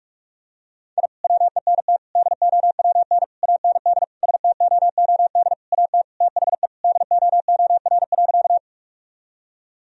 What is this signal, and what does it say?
Morse code, heard as a keyed tone, about 46 wpm, 700 Hz. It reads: I WENT DOWN AND STOOD AT THE DOOR.
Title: Sentences from Top 1000 Words - Review Part 2 - 45wpm